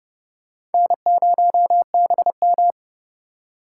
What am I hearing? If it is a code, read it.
Morse code, heard as a keyed tone, about 30 wpm, 700 Hz. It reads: N0BM
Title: USA Call Signs - 8x Character Spacing - 30wpm